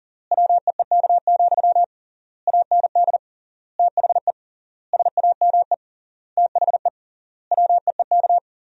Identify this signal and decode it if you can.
Morse code, heard as a keyed tone, about 40 wpm, 700 Hz. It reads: WEEK, AND THE SAME THE WEEK
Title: 3x Word-Spacing - Sentences from Top 500 Words - 40wpm